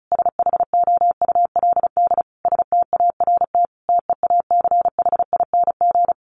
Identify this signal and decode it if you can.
Morse code, heard as a keyed tone, about 34 wpm, 700 Hz. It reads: SHOULD START TEACHING
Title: Sentences from Top 1100 Words 35wpm